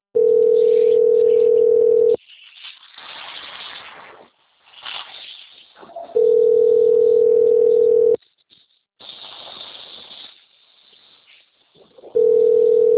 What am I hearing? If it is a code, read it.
Ringback tone (North America), 440 + 480 Hz.